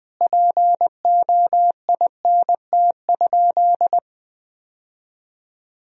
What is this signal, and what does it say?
Morse code, heard as a keyed tone, about 21 wpm, 700 Hz. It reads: POINT?
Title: Sentences from Top 1000 Words - Review Part 1 - 20wpm